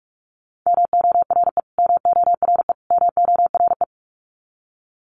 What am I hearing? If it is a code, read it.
Morse code, heard as a keyed tone, about 48 wpm, 700 Hz. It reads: MORE MORE MORE
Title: Top 500 Words Repeated 45wpm